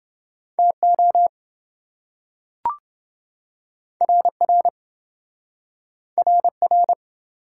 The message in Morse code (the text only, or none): TO RR RR